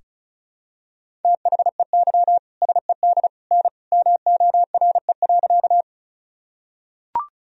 Morse code, heard as a keyed tone, about 34 wpm, 700 Hz. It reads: THEY SED N MORE.
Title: Sentences from Top 100 Words and Abbreviations 35wpm